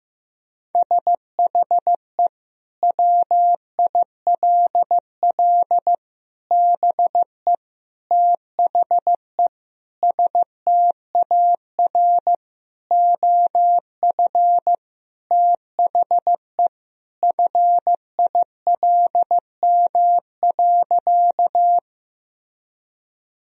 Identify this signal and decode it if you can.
Morse code, heard as a keyed tone, about 15 wpm, 700 Hz. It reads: SHE WILL BE THE STAR OF THE FILM.
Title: Sentences from Top 1000 Words - Review Part 3 - 15wpm